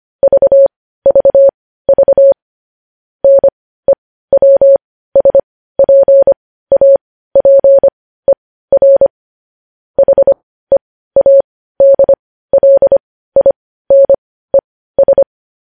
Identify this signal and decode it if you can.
Morse code, heard as a keyed tone, about 25 wpm, spaced as for 15 wpm, 550 Hz. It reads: VVV NEWSPAPER HEADLINES
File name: podcast / News headlines-1x-25x15